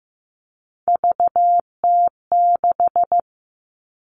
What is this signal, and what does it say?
Morse code, heard as a keyed tone, about 15 wpm, 700 Hz. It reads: VT6
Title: All Cut Numbers - 3-Digit 15wpm